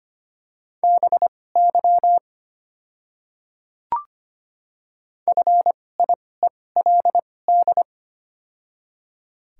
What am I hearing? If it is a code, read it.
Morse code, heard as a keyed tone, about 25 wpm, spaced as for 18 wpm, 700 Hz. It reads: BY FIELD
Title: 2x Character Spacing - Top 500 Words - 25wpm